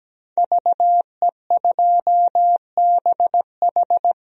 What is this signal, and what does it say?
Morse code, heard as a keyed tone, about 17 wpm, 700 Hz. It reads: VE2BH
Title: DXCC-100 Call Signs - 5 Characters - 17wpm